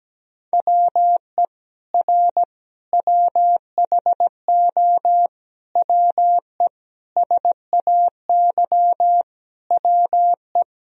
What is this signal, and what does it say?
Morse code, heard as a keyed tone, about 17 wpm, 700 Hz. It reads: WE R WHO WE SAY WE
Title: Sentences from Top 200 Words and Abbreviations 17wpm